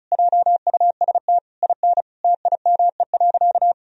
Morse code, heard as a keyed tone, about 36 wpm, 700 Hz. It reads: JUST IN TIME.
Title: Sentences from Top 1000 Words - Review Part 2 - 35wpm